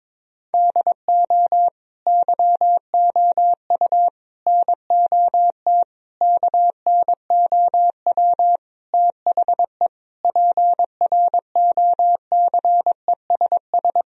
Morse code, heard as a keyed tone, about 22 wpm, 700 Hz. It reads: DO YOU NOT KNOW THE PROCESS
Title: Sentences from Top 1000 Words - Review Part 3 - 22wpm